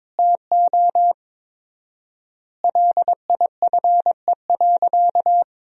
Morse code, heard as a keyed tone, about 22 wpm, 700 Hz. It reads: TO LIFE.
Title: 4x Word-Spacing - Sentences from Top 400 Words - 22wpm